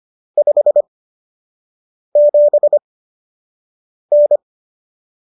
Morse code, heard as a keyed tone, about 25 wpm, spaced as for 6 wpm, 590 Hz. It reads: 57N